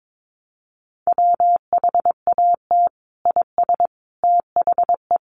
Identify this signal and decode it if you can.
Morse code, heard as a keyed tone, about 22 wpm, 700 Hz. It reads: WHAT IS THE
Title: Sentences from Top 900 Words 22wpm